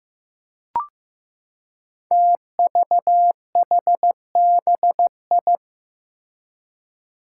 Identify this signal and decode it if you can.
Morse code, heard as a keyed tone, about 15 wpm, 700 Hz. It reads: TVHBI
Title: Five Challenging Letter-Number Combinations 15wpm